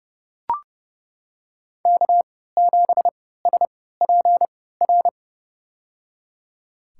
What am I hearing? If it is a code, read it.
Morse code, heard as a keyed tone, about 30 wpm, spaced as for 17 wpm, 700 Hz. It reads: K7SPR
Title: USA Call Signs - 5 Characters - 3x Character Spacing - 30wpm